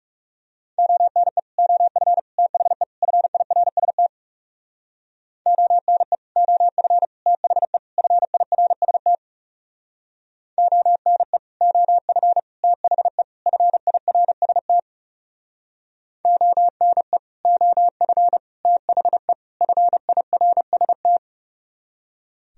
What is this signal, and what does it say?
Morse code, heard as a keyed tone, about 37 wpm, 700 Hz. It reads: ONE OF THE FIRST ONE OF THE FIRST ONE OF THE FIRST ONE OF THE FIRST
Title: Speed-Racing - Sets of 4 Words 30wpm